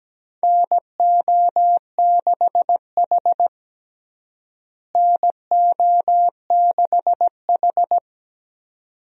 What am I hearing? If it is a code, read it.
Morse code, heard as a keyed tone, about 17 wpm, 700 Hz. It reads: NO6H NO6H
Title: USA Call Signs - 4 Characters - Repeated 2x - 17wpm